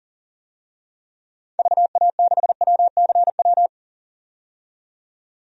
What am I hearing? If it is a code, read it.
Morse code, heard as a keyed tone, about 40 wpm, 700 Hz. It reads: VA6WCW
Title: DXCC-100 Call Signs - 6 Characters - 40wpm